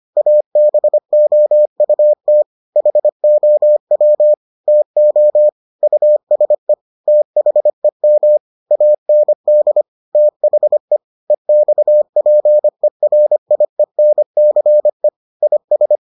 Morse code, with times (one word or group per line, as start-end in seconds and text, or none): 0.2-2.4 ABOUT
2.8-4.3 HOW
4.7-5.5 TO
5.8-6.7 USE
7.1-8.4 THEM
8.7-9.8 AND
10.2-11.0 THE
11.3-15.1 EXPERIENCE
15.4-16.0 IS